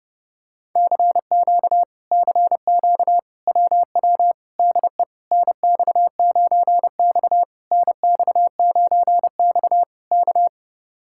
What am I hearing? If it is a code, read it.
Morse code, heard as a keyed tone, about 31 wpm, 700 Hz. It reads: CQ CQ WW DE NX9X NX9X K